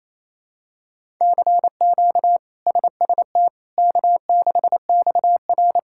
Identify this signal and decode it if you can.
Morse code, heard as a keyed tone, about 28 wpm, 700 Hz. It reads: CQ SST K6XR
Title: K1USN SST QSO Practice - 28wpm